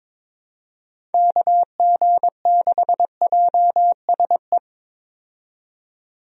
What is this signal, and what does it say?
Morse code, heard as a keyed tone, about 22 wpm, 700 Hz. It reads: KG6JSE
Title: USA Call Signs - 6 Characters - 22wpm